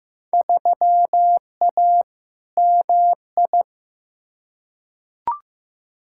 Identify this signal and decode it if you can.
Morse code, heard as a keyed tone, about 15 wpm, 700 Hz. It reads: 3A MI
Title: Field Day Exchanges - 15wpm